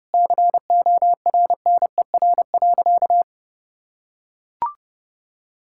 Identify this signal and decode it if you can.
Morse code, heard as a keyed tone, about 32 wpm, 700 Hz. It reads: CORNER.
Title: Sentences from Top 1600 Words 30wpm